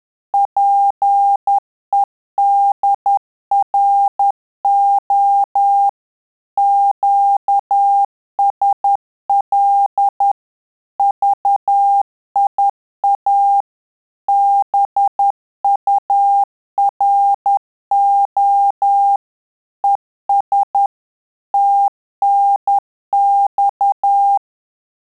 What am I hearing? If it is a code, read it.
Morse code, heard as a keyed tone, about 10 wpm, 800 Hz. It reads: PEDRO QSL VIA BURO ES TNX
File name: Telegraphy-group / QSO CW 50 CPM